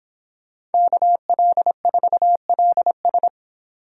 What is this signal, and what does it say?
Morse code, heard as a keyed tone, about 30 wpm, 700 Hz. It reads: KL4LS